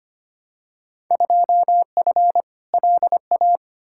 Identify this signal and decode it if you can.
Morse code, heard as a keyed tone, about 25 wpm, 700 Hz. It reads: 2F LA
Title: Field Day Exchanges - 25wpm